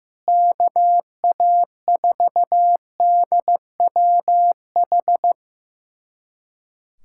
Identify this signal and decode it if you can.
Morse code, heard as a keyed tone, about 15 wpm, 700 Hz. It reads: KA4DWH
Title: USA Callsign DE Callsign - 6 Characters - 15wpm